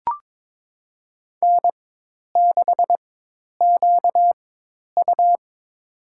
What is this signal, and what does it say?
Morse code, heard as a keyed tone, about 22 wpm, spaced as for 10 wpm, 700 Hz. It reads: N6QU
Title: USA Call Signs - 4x Character Spacing - 22wpm